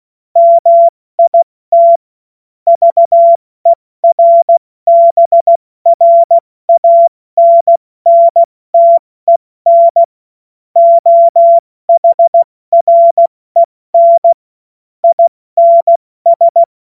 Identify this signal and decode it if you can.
Morse code, heard as a keyed tone, about 16 wpm, 680 Hz. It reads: MIT VERBRANNTEN OHREN INS
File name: Witz-09-September-2025-70-F80